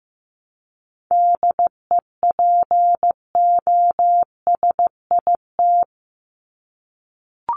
Morse code, heard as a keyed tone, about 15 wpm, 700 Hz. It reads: DEPOSIT
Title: Top 1251-1500 Words 15wpm